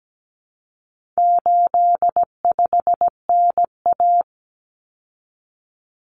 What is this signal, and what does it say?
Morse code, heard as a keyed tone, about 17 wpm, 700 Hz. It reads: 85NA
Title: Common Cut Numbers - 4-Digit 17wpm